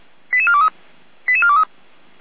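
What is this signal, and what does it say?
Five-tone selective call (ZVEI), heard as quick runs of five tones, tones of about 70 ms. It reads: 88423 88423
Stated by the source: Five-tone ZVEI selective call received on an SDR.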